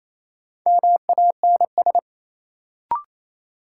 Morse code, heard as a keyed tone, about 28 wpm, 700 Hz. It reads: MANS
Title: Sentences from Top 1000 Words - Review Part 6 - New Word Drill - 28wpm